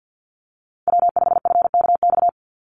Morse code, heard as a keyed tone, about 50 wpm, 700 Hz. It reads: W5FKX
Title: Call Signs - 50wpm